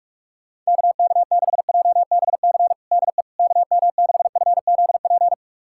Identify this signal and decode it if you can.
Morse code, heard as a keyed tone, about 44 wpm, 700 Hz. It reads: KK6JBC DE KM6FZP
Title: USA Callsign DE Callsign - 6 Characters - 45wpm